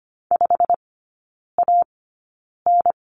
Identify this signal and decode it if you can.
Morse code, heard as a keyed tone, about 25 wpm, spaced as for 9 wpm, 700 Hz. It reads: 5AN
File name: Lesson 17e - Letter u - Three Characters 25wpm